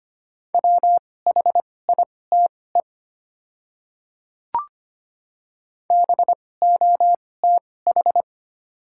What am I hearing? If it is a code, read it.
Morse code, heard as a keyed tone, about 25 wpm, spaced as for 18 wpm, 690 Hz. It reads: WHITE BOTH
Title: 2x Character Spacing - Top 500 Words - 25wpm